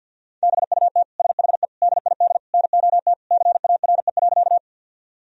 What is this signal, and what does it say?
Morse code, heard as a keyed tone, about 53 wpm, 700 Hz. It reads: BUT SHE DID NOT CARE.
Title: Sentences from Top 1000 Words - Review Part 1 - 50wpm